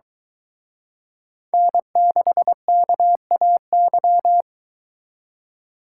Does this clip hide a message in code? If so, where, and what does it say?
Yes: Morse code from 1.5 to 4.4 s, heard as a keyed tone, about 23 wpm, 700 Hz. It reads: N6KAY